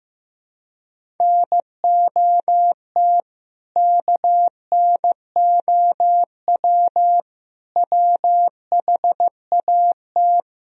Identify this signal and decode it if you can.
Morse code, heard as a keyed tone, about 15 wpm, 700 Hz. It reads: NOT KNOW WHAT
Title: Speed-Racing - Sets of 5 Words 15wpm